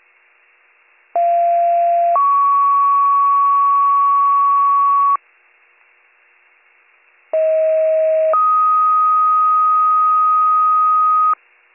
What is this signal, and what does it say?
Two-tone page, heard as a one-second tone, then a three-second tone, about 690 then 1120 Hz, about 640 then 1250 Hz. It is heard from 1.2 to 11.3 s.